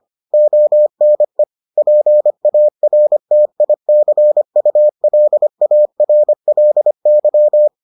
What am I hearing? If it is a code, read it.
Morse code, heard as a keyed tone, about 25 wpm, 600 Hz. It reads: ONE PARTICULARLY